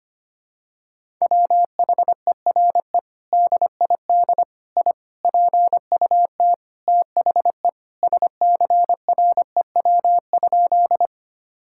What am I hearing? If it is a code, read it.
Morse code, heard as a keyed tone, about 25 wpm, 700 Hz. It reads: WHERE DID I PUT THE SCREW?
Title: Sentences from Top 1600 Words 25wpm